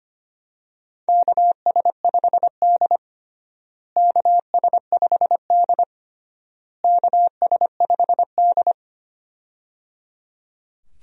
Morse code, heard as a keyed tone, about 25 wpm, 700 Hz. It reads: KS5D KS5D KS5D
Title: USA Call Signs - 4 Characters - Repeated 3x - 25wpm